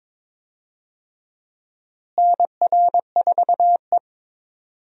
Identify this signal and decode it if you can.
Morse code, heard as a keyed tone, about 22 wpm, 700 Hz. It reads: NR4E
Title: USA Call Signs - 4 Characters - Repeated 3x - 22wpm